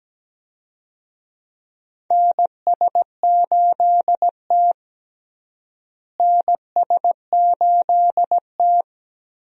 Morse code, heard as a keyed tone, about 17 wpm, 700 Hz. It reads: NS8T NS8T
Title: USA Call Signs - 4 Characters - Repeated 2x - 17wpm